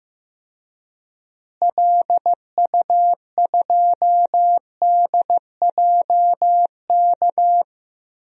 Morse code, heard as a keyed tone, about 15 wpm, 700 Hz. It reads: LU2DJK
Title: DXCC-100 Call Signs - 6 Characters - Repeated 3x - 15wpm